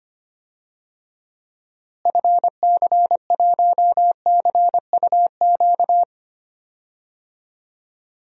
Morse code, heard as a keyed tone, about 25 wpm, 700 Hz. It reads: FC1CUQ
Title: DXCC-100 Call Signs - 6 Characters - Repeated 3x - 25wpm